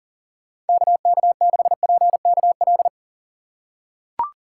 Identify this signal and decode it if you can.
Morse code, heard as a keyed tone, about 40 wpm, 700 Hz. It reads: KK6PKL
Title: USA Call Signs - 6 Characters - 3x Character Spacing - 40wpm